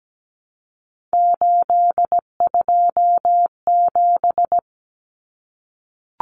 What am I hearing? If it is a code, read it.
Morse code, heard as a keyed tone, about 17 wpm, 700 Hz. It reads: 827
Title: Common Cut Numbers - 3-Digit 17wpm